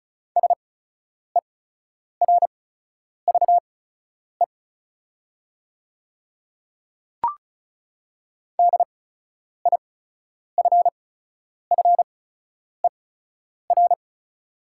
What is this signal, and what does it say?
Morse code, heard as a keyed tone, about 36 wpm, spaced as for 10 wpm, 700 Hz. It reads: SERVE DIFFER
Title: 8x Character Spacing - Top 500 Words - 35wpm